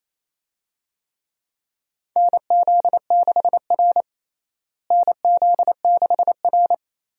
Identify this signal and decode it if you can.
Morse code, heard as a keyed tone, about 28 wpm, 700 Hz. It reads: NZ6R NZ6R
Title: USA Call Signs - 4 Characters - Repeated 2x - 28wpm